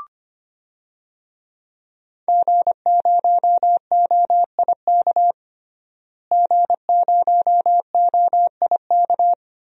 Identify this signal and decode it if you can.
Morse code, heard as a keyed tone, about 25 wpm, 700 Hz. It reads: G0OIK G0OIK
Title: DXCC-100 Call Signs - 5 Characters - Repeated 2x - 25wpm